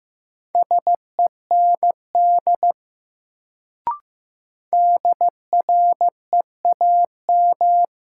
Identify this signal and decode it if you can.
Morse code, heard as a keyed tone, about 15 wpm, 700 Hz. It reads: SEND DREAM
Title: Sentences from Top 1500 Words - New Word Drill - 15wpm